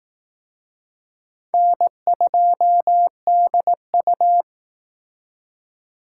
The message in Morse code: N2DU

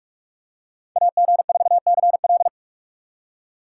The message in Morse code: AG4CL

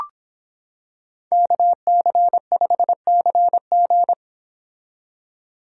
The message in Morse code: KC5CG

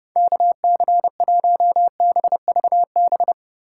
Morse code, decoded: KC1BVB

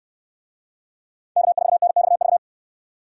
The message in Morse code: K4NXV

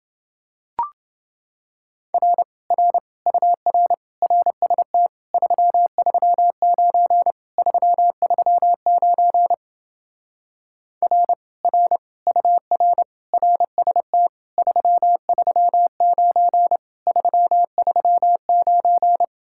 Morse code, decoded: R R UR RST 339 339 R R UR RST 339 339

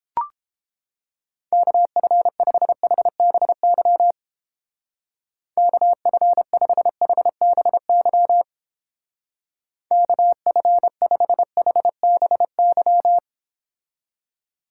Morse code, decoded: KF5HBY KF5HBY KF5HBY

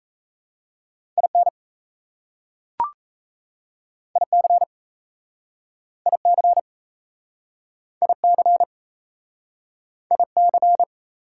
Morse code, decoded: IN IC IC IC IC

